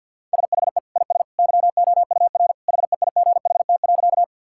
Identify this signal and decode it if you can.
Morse code, heard as a keyed tone, about 50 wpm, 700 Hz. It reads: SHE IS YOUR HEIGHT.
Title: Sentences from Top 1600 Words 50wpm